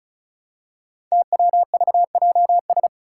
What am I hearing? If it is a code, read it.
Morse code, heard as a keyed tone, about 36 wpm, 700 Hz. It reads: TWVJS
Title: Five Challenging Letter-Number Combinations 35wpm